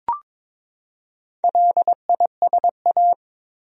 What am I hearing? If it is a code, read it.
Morse code, heard as a keyed tone, about 21 wpm, 700 Hz. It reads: LISA
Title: USA Common Names - Max 5 Letters - 22wpm